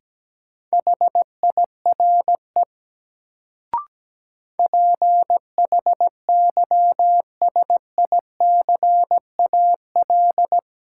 Morse code, heard as a keyed tone, about 17 wpm, 700 Hz. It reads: HIRE PHYSICAL
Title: Sentences from Top 1400 Words - New Word Drill Alt - 17wpm